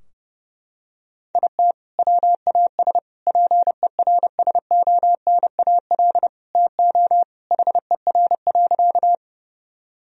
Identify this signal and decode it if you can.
Morse code, heard as a keyed tone, about 30 wpm, 700 Hz. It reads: IT WAS PERSONAL TO HER.
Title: Sentences from Top 1000 Words - Review Part 4 - 30wpm